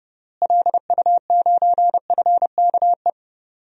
Morse code, heard as a keyed tone, about 32 wpm, 700 Hz. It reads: LU9FKE
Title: DXCC-100 Call Signs - 6 Characters - Repeated 3x - 30wpm